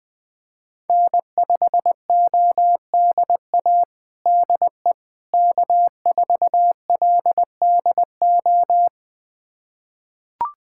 Morse code, heard as a keyed tone, about 21 wpm, 700 Hz. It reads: N5ODA DE K4LDO
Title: USA Callsign DE Callsign - 5 Characters - 20wpm